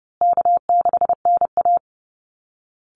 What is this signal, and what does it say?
Morse code, heard as a keyed tone, about 30 wpm, 700 Hz. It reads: K6NA
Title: Easy Call Signs Repeated 3x - 30wpm